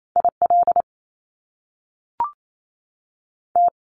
Morse code, heard as a keyed tone, about 28 wpm, spaced as for 3 wpm, 700 Hz. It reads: I LT
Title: Top Two Letter Combinations in Words 28wpm